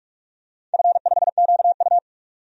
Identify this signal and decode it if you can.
Morse code, heard as a keyed tone, about 44 wpm, 700 Hz. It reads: F5QU
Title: DXCC-100 Call Signs - 4 Characters - 45wpm